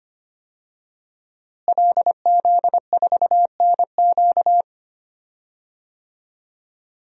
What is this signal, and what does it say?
Morse code, heard as a keyed tone, about 25 wpm, 700 Hz. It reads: LZ4NQ
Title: DXCC-100 Call Signs - 5 Characters - Repeated 3x - 25wpm